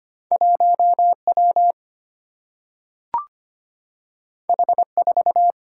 Morse code, heard as a keyed tone, about 25 wpm, 700 Hz. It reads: 1W H4
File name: Two Letter-Number Combinations - 25wpm